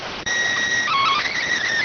SSTV (slow-scan television), PD 290.